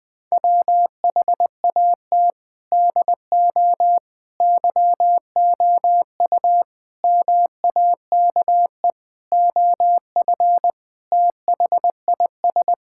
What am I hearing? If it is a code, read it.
Morse code, heard as a keyed tone, about 20 wpm, 700 Hz. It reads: WHAT DO YOU MAKE OF THIS
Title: Sentences with 5-Word Phrases - Spoken Beginning - 20wpm